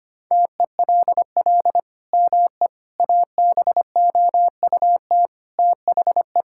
Morse code, heard as a keyed tone, about 25 wpm, 700 Hz. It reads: TELL ME ABOUT THE
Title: Sentences from Top 1000 Words - Review Part 6 - 25wpm